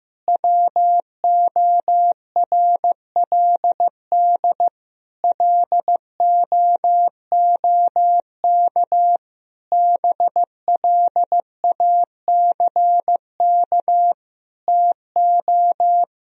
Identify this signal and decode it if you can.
Morse code, heard as a keyed tone, about 16 wpm, 700 Hz. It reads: WORLD LOOK BLACK TO
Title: Sentences from Top 1000 Words - Review Part 2 - 15wpm